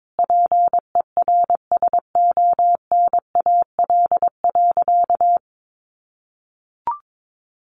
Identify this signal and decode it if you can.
Morse code, heard as a keyed tone, about 22 wpm, 700 Hz. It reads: PERSONAL.
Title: Sentences from Top 700 Words 22wpm